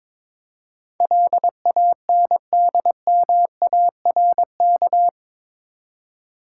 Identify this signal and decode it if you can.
Morse code, heard as a keyed tone, about 22 wpm, 700 Hz. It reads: LANDMARK
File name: practice / 22wpm / Eight Letter Words - 22wpm